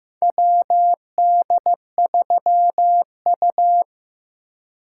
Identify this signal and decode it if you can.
Morse code, heard as a keyed tone, about 15 wpm, 700 Hz. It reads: WD3U
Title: USA Call Signs - 3x Character Spacing - 15wpm